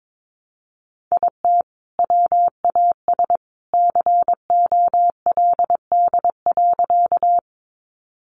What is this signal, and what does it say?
Morse code, heard as a keyed tone, about 22 wpm, 700 Hz. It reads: IT WAS COLD.
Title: Sentences from Top 400 Words 22wpm